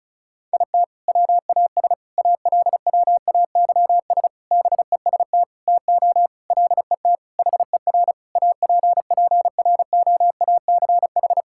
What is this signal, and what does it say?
Morse code, heard as a keyed tone, about 36 wpm, 700 Hz. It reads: IT WAS ALWAYS BEST TO LET HER APPROACH
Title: Sentences from Top 1000 Words - Review Part 6 - 35wpm